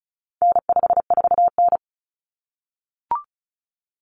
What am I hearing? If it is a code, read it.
Morse code, heard as a keyed tone, about 34 wpm, 700 Hz. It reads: N54N
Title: Common Cut Numbers - 4-Digit 35wpm